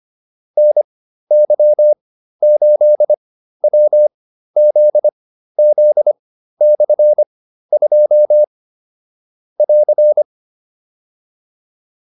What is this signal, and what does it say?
Morse code, heard as a keyed tone, about 25 wpm, spaced as for 13 wpm, 600 Hz. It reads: NY8WZZ/2 <AR>
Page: intermediate-practice-files